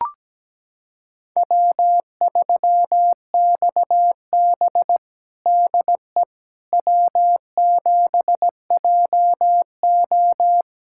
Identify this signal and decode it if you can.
Morse code, heard as a keyed tone, about 17 wpm, 700 Hz. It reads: W3XB DE W7JO
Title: USA Callsign DE Callsign - 4 Characters - 17wpm